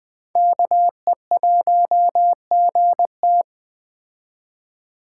Morse code, heard as a keyed tone, about 20 wpm, 700 Hz. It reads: KE1GT
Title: Speed Racing - USA 2x2 Call Signs 17wpm